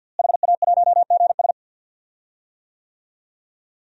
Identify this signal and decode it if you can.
Morse code, heard as a keyed tone, about 51 wpm, 700 Hz. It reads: HA1GS
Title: DXCC-100 Call Signs - 5 Characters - Repeated 2x - 50wpm